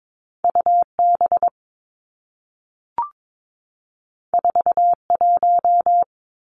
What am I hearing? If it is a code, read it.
Morse code, heard as a keyed tone, about 22 wpm, 700 Hz. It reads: U B 4 1
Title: Two Challenging Letter-Number Combinations 22wpm